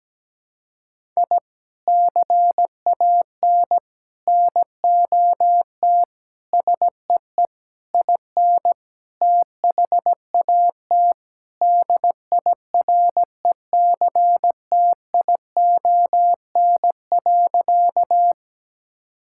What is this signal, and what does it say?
Morse code, heard as a keyed tone, about 17 wpm, 700 Hz. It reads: I CAN NOT SEE IN THAT DIRECTION.
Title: Sentences from Top 1000 Words - Review Part 5 - 17wpm